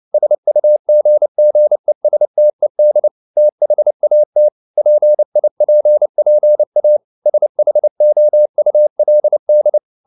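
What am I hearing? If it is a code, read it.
Morse code, heard as a keyed tone, about 29 wpm, 600 Hz. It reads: SUGGESTED THAT PIPPA SHOULD
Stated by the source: Morse code at 29 wpm, sending Ken Follett's novel Never in English, from CW med Gnister.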